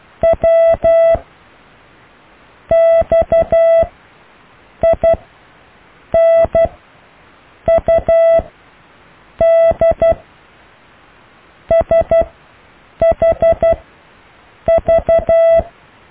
Morse code, heard as a keyed tone, about 12 wpm, 660 Hz. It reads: W X I N U D S H V
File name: cw / Alle letters